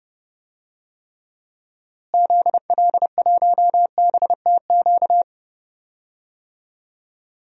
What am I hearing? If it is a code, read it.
Morse code, heard as a keyed tone, about 32 wpm, 700 Hz. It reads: ZL1BTQ
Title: DXCC-100 Call Signs - 6 Characters - Repeated 2x - 30wpm